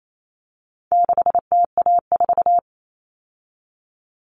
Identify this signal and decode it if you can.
Morse code, heard as a keyed tone, about 28 wpm, 700 Hz. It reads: 6TA4
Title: All Cut Numbers - 4-Digit 28wpm